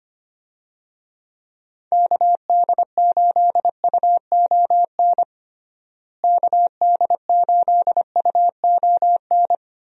Morse code, heard as a keyed tone, about 25 wpm, 700 Hz. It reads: KD8UON KD8UON
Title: USA Call Signs - 6 Characters - Repeated 2x - 25wpm